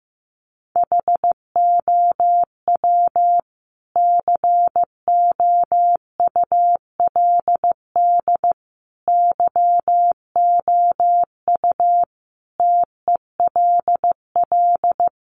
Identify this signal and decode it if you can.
Morse code, heard as a keyed tone, about 15 wpm, 700 Hz. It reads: HOW COULD YOU TELL
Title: Sentences from Top 1000 Words 15wpm